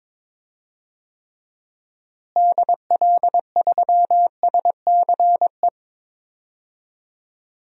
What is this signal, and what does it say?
Morse code, heard as a keyed tone, about 22 wpm, 700 Hz. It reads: DL3SCE
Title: DXCC-100 Call Signs - 6 Characters - Repeated 3x - 22wpm